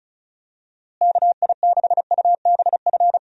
Morse code, heard as a keyed tone, about 34 wpm, 700 Hz. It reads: KI6UBF
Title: USA Call Signs - 6 Characters - Repeated 2x - 35wpm